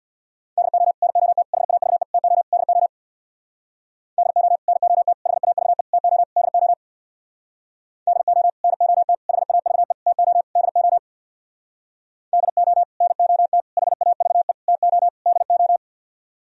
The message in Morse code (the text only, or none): DO NOT HAVE TO DO DO NOT HAVE TO DO DO NOT HAVE TO DO DO NOT HAVE TO DO